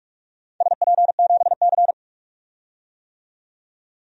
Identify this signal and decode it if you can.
Morse code, heard as a keyed tone, about 46 wpm, 700 Hz. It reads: SP7C
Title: DXCC-100 Call Signs - 4 Characters - 45wpm